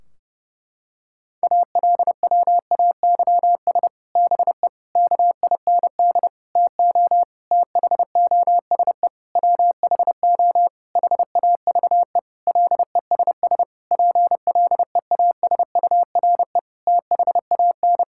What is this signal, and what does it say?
Morse code, heard as a keyed tone, about 30 wpm, 700 Hz. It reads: ALWAYS BE KIND TO THOSE WHO HAVE LESS PLEASURE THAN
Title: Sentences from Top 1600 Words 30wpm